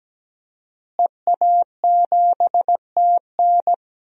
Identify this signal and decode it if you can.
Morse code, heard as a keyed tone, about 17 wpm, 700 Hz. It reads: EA7TN
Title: DXCC-100 Call Signs - 5 Characters - Repeated 2x - 17wpm